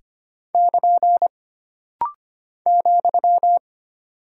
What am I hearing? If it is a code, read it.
Morse code, heard as a keyed tone, about 25 wpm, 700 Hz. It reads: <KN> ,
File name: Single Punctuation-Prosign - Rapid-Fire - 25wpm